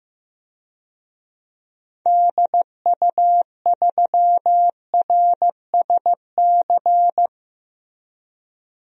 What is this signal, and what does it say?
Morse code, heard as a keyed tone, about 15 wpm, 700 Hz. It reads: DU3RSC
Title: DXCC-100 Call Signs - 6 Characters - 15wpm